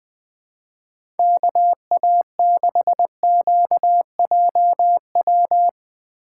Morse code, heard as a keyed tone, about 20 wpm, 700 Hz. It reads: KA6QJW
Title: USA Call Signs - 2x Character Spacing - 20wpm